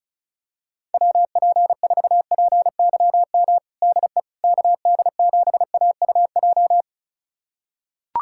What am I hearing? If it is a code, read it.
Morse code, heard as a keyed tone, about 36 wpm, 700 Hz. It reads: WP4PYM DE KD7AUJ
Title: USA Callsign DE Callsign - 6 Characters - 35wpm